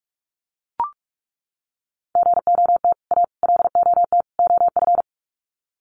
Morse code, heard as a keyed tone, about 44 wpm, 700 Hz. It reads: GOT A LOT OF
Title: Sets of 4 Words 45wpm